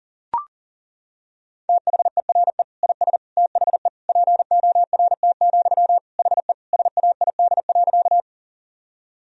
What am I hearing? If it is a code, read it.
Morse code, heard as a keyed tone, about 37 wpm, 700 Hz. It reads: THERE IS THE PORT, HE SAID.
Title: Sentences from Top 1000 Words - Review Part 1 - 40wpm